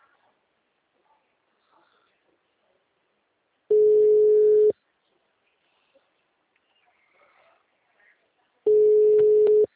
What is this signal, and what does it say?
Ringback tone (Europe, 425 Hz).
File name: out- -2016- - - .314